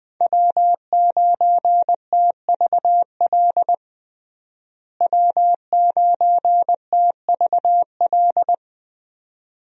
Morse code, heard as a keyed tone, about 20 wpm, 700 Hz. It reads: W9TVL W9TVL
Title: USA Call Signs - 5 Characters - Repeated 2x - 20wpm